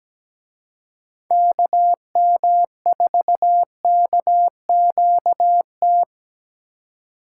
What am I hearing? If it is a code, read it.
Morse code, heard as a keyed tone, about 20 wpm, 700 Hz. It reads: KM4KQT
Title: Speed Racing - USA 2x3 Call Signs 17wpm